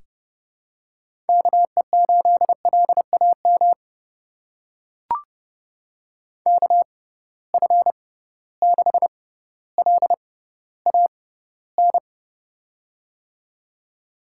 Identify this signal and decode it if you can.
Morse code, heard as a keyed tone, about 29 wpm, spaced as for 15 wpm, 700 Hz. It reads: KE8LAM KF6LAN
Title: USA Call Signs - 6 Characters - 6x Character Spacing - 30wpm